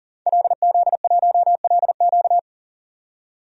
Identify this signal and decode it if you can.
Morse code, heard as a keyed tone, about 40 wpm, 700 Hz. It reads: LZ1LQ
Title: DXCC-100 Call Signs - 5 Characters - 40wpm